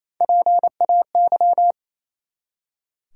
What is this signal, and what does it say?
Morse code, heard as a keyed tone, about 28 wpm, 700 Hz. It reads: PAY
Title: Three Letter Words - 28wpm